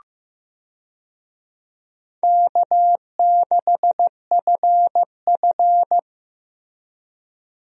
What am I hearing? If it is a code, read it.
Morse code, heard as a keyed tone, about 15 wpm, 700 Hz. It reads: K6FF
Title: USA Call Signs - 4 Characters - Repeated 2x - 15wpm